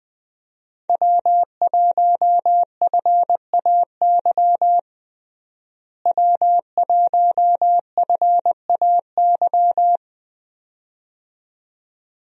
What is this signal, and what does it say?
Morse code, heard as a keyed tone, about 20 wpm, 700 Hz. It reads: W1FAY W1FAY
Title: USA Call Signs - 5 Characters - Repeated 2x - 20wpm